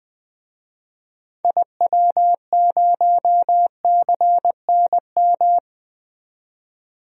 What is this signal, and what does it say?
Morse code, heard as a keyed tone, about 20 wpm, 700 Hz. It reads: IW0CNM
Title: DXCC-100 Call Signs - 6 Characters - 20wpm